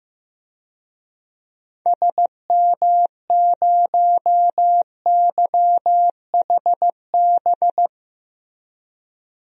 Morse code, heard as a keyed tone, about 15 wpm, 700 Hz. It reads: SM0YHB
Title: DXCC-100 Call Signs - 6 Characters - Repeated 2x - 15wpm